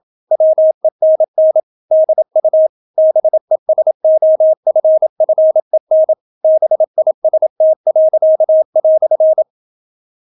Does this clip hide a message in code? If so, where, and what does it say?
Yes: Morse code from 0.3 to 9.4 s, heard as a keyed tone, about 27 wpm, 630 Hz. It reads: WENN DU BESOFFEN BIST."